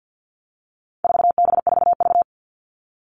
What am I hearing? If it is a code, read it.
Morse code, heard as a keyed tone, about 50 wpm, 700 Hz. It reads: 4B4V
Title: All Cut Numbers - 4-Digit 50wpm